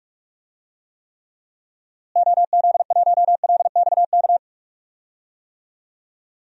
Morse code, heard as a keyed tone, about 46 wpm, 700 Hz. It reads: OZ1LXK
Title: DXCC-100 Call Signs - 6 Characters - Repeated 2x - 45wpm